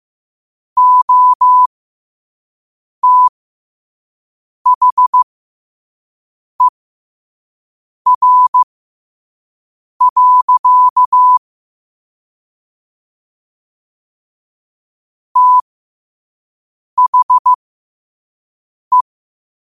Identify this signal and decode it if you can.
Morse code, heard as a keyed tone, about 15 wpm, spaced as for 5 wpm, 1000 Hz. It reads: OTHER. THE
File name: qotdfiles / 5wpm